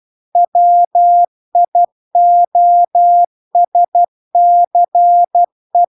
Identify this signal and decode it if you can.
Morse code, heard as a keyed tone, about 12 wpm, 700 Hz. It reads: WIOSCE